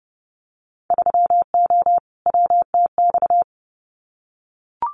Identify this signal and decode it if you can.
Morse code, heard as a keyed tone, about 30 wpm, 700 Hz. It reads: 3O WTX
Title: Winter Field Day Exchanges Encore 30wpm